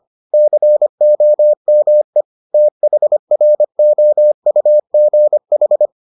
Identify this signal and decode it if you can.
Morse code, heard as a keyed tone, about 25 wpm, 600 Hz. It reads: COME THROUGH